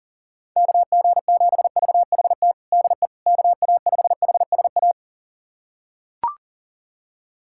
Morse code, heard as a keyed tone, about 37 wpm, 700 Hz. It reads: KG7VHT DE KA5HSA
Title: USA Callsign DE Callsign - 6 Characters - 40wpm